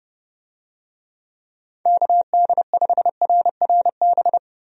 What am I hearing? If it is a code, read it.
Morse code, heard as a keyed tone, about 30 wpm, 700 Hz. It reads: KD5RRB
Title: USA Call Signs - 6 Characters - Repeated 2x - 30wpm